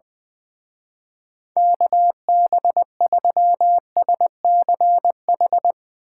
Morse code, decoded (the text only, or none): KB3SCH